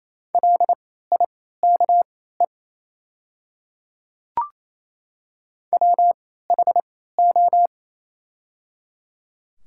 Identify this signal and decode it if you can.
Morse code, heard as a keyed tone, about 27 wpm, spaced as for 16 wpm, 690 Hz. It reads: LIKE WHO
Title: 3x Character Spacing - Top 100 Words - 28wpm